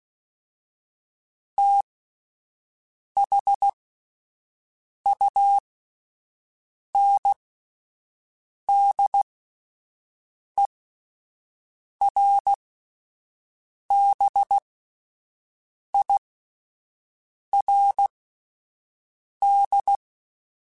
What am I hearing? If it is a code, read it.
Morse code, heard as a keyed tone, about 16 wpm, spaced as for 5 wpm, 800 Hz. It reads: THUNDERBIRD
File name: QSOs / QSOs-mp3s / WD0BGP 1@13